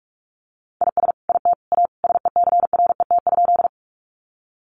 Morse code, heard as a keyed tone, about 44 wpm, 700 Hz. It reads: IS IT A SECRET?
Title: Sentences from Top 1200 Words 45wpm